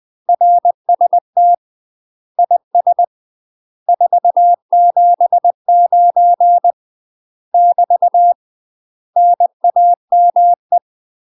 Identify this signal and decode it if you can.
Morse code, heard as a keyed tone, about 20 wpm, 700 Hz. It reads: RST IS 479 <BT> NAME
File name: UK MASTER QSOs TRAINING / 20WPM / UK MASTER QSOs 0003